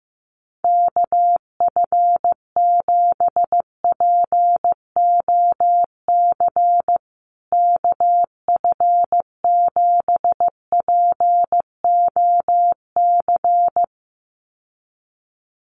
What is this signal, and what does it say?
Morse code, heard as a keyed tone, about 15 wpm, 700 Hz. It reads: KF7POC KF7POC
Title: USA Call Signs Repeated 2x 15wpm